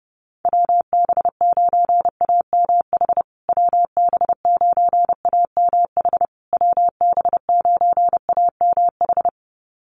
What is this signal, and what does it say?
Morse code, heard as a keyed tone, about 30 wpm, 700 Hz. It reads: WB9AMH WB9AMH WB9AMH
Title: USA Call Signs Repeated 3x 30wpm